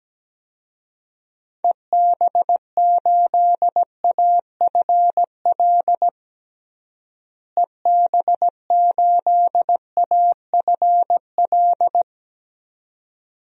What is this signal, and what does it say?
Morse code, heard as a keyed tone, about 17 wpm, 700 Hz. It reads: EB8AFL EB8AFL
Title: DXCC-100 Call Signs - 6 Characters - Repeated 2x - 17wpm